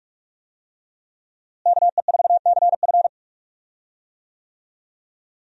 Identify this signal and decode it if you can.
Morse code, heard as a keyed tone, about 45 wpm, 700 Hz. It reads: KE4CF
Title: USA Call Signs - 5 Characters - Repeated 2x - 45wpm